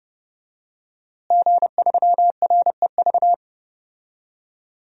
Morse code, heard as a keyed tone, about 28 wpm, 700 Hz. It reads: G3REV
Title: DXCC-100 Call Signs - 5 Characters - Repeated 3x - 30wpm